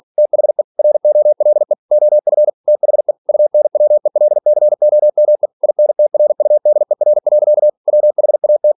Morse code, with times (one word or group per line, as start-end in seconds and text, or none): 0.2-0.6 THE
0.8-1.7 ROLE
1.9-2.5 OF
2.7-3.1 THE
3.3-5.5 UNWELCOME
5.6-7.7 INTRUDER.
7.9-8.7 WHAT